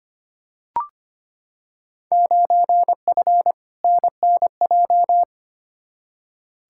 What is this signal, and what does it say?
Morse code, heard as a keyed tone, about 25 wpm, 700 Hz. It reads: 9F NNJ
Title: Field Day Exchanges Encore - 25wpm